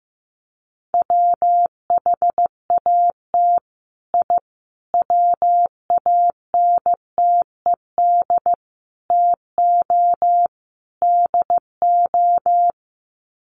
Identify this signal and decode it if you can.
Morse code, heard as a keyed tone, about 15 wpm, 700 Hz. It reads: WHAT I WANTED TO DO
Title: Sets of 5 Words Encore2 15wpm